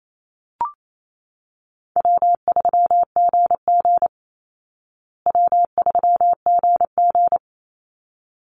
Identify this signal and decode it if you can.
Morse code, heard as a keyed tone, about 28 wpm, 700 Hz. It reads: W3GG W3GG
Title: Call Signs Repeated 3x - 28wpm